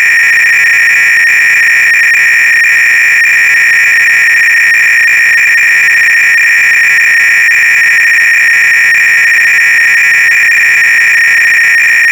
300-baud modem data.